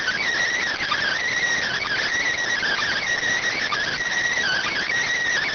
Emergency Alert System (SAME) tones, heard as harsh buzzing bursts. It is heard from 0.7 to 4.8 s.